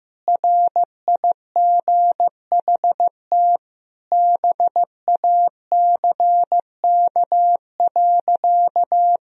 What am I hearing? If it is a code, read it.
Morse code, heard as a keyed tone, about 16 wpm, 700 Hz. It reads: RIGHT BACK.
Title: Sentences from Top 1000 Words - Review Part 4 - 15wpm